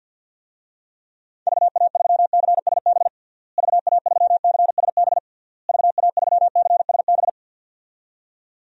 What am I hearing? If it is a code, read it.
Morse code, heard as a keyed tone, about 51 wpm, 700 Hz. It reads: VA3CSB VA3CSB VA3CSB
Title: DXCC-100 Call Signs - 6 Characters - Repeated 3x - 50wpm